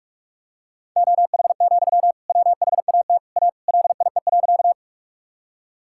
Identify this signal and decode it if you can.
Morse code, heard as a keyed tone, about 46 wpm, 700 Hz. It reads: OH, WHAT A LIE.
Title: Sentences from Top 1600 Words 45wpm